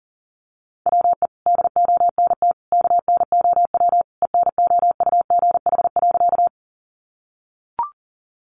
Morse code, heard as a keyed tone, about 40 wpm, 700 Hz. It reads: WE DONT KNOW ENOUGH.
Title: Sentences from Top 400 Words 40wpm